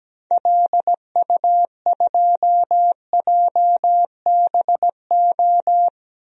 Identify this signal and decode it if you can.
Morse code, heard as a keyed tone, about 17 wpm, 700 Hz. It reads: LU2JBO
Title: DXCC-100 Call Signs - 6 Characters - 17wpm